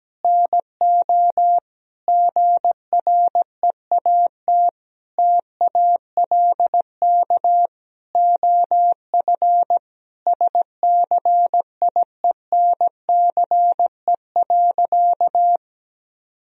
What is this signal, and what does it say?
Morse code, heard as a keyed tone, about 17 wpm, 700 Hz. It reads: NO GREAT TALK OF SCIENCE.
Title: Sentences from Top 1000 Words - Review Part 2 - 17wpm